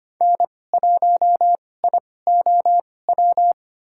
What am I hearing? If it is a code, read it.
Morse code, heard as a keyed tone, about 25 wpm, spaced as for 18 wpm, 700 Hz. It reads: N1IOW
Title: USA Call Signs - 5 Characters - 2x Character Spacing - 25wpm